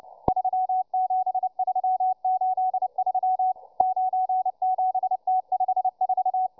Morse code, heard as a keyed tone, about 29 wpm, 750 Hz, with radio noise. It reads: 37383 97T54